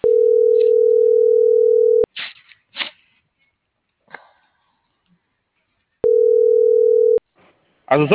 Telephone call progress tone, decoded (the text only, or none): ringback tone